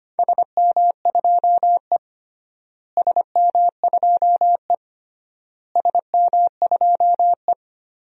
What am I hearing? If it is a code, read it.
Morse code, heard as a keyed tone, about 25 wpm, 700 Hz. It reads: SM2E SM2E SM2E
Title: DXCC-100 Call Signs - 4 Characters - Repeated 3x - 25wpm